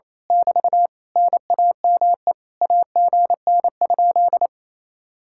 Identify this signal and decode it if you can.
Morse code, heard as a keyed tone, about 26 wpm, 700 Hz. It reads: <BT> NAME AGN?
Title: Speed-Racing Common QSO Phrases Encore - 28wpm